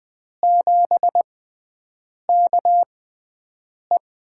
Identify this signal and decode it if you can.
Morse code, heard as a keyed tone, about 20 wpm, spaced as for 7 wpm, 700 Hz. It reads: 7KE
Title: USA Call Signs - 6 Characters - 6x Character Spacing - 20wpm